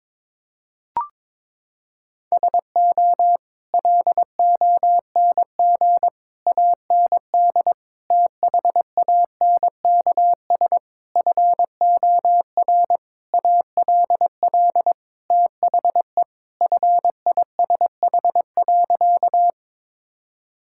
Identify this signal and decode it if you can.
Morse code, heard as a keyed tone, about 21 wpm, 700 Hz. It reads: SO LONG AND THANKS FOR ALL THE FISH.
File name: practice / 22wpm / Sentences with 8-Word Phrases - 22wpm